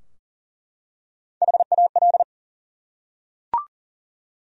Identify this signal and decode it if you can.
Morse code, heard as a keyed tone, about 40 wpm, 700 Hz. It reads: HAL
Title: USA Common Names - Max 5 Letters - 40wpm